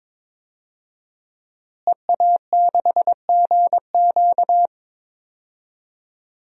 Morse code, heard as a keyed tone, about 21 wpm, 700 Hz. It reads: EA6GQ
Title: DXCC-100 Call Signs - 5 Characters - Repeated 2x - 22wpm